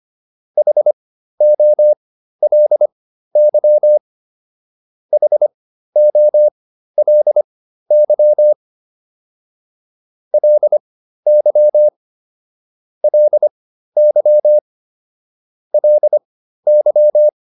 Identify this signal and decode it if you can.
Morse code, heard as a keyed tone, about 25 wpm, spaced as for 13 wpm, 600 Hz. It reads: HOLY HOLY LY LY LY